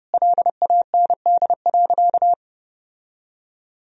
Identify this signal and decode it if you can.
Morse code, heard as a keyed tone, about 32 wpm, 700 Hz. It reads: LAND.
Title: Sentences from Top 1000 Words - Review Part 2 - 30wpm